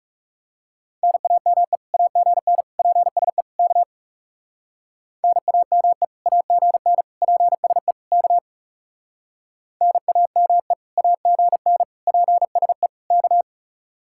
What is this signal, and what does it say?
Morse code, heard as a keyed tone, about 36 wpm, 700 Hz. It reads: NAME AGN PSE K NAME AGN PSE K NAME AGN PSE K
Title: Speed-Racing Common QSO Phrases - 30wpm